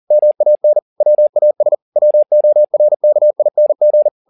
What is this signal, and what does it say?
Morse code, heard as a keyed tone, about 40 wpm, 600 Hz. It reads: MAN WAS WORKING